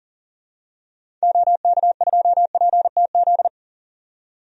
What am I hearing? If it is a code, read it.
Morse code, heard as a keyed tone, about 39 wpm, 700 Hz. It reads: OK2PTZ
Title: DXCC-100 Call Signs - 6 Characters - 40wpm